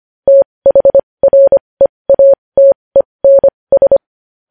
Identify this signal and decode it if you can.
Morse code, heard as a keyed tone, about 25 wpm, spaced as for 20 wpm, 550 Hz. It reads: THREATENS
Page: news-headlines-in-morse-code-at-20-wpm